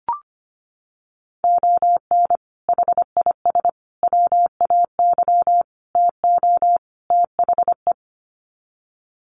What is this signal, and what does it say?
Morse code, heard as a keyed tone, about 25 wpm, 700 Hz. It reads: ON HIS WAY TO THE
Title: Sets of 5 Words Encore2 25wpm